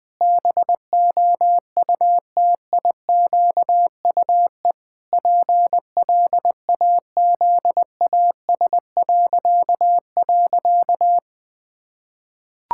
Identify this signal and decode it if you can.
Morse code, heard as a keyed tone, about 21 wpm, 700 Hz. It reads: BOUTIQUE PLAZAS..
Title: Pangram Challenge v2 - 20wpm